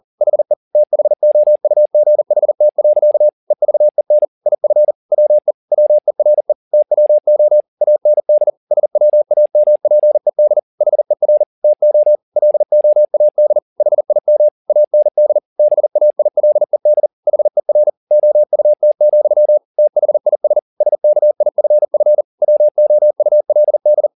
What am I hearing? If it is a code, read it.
Morse code, heard as a keyed tone, about 40 wpm, 600 Hz. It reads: HE THOUGHT. EVEN IF WE WERE TWO AND SWAMPED HER TO LOAD HIM AND BAILED HER OUT, THIS SKIFF WOULD
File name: the old man and the sea 2 40wpm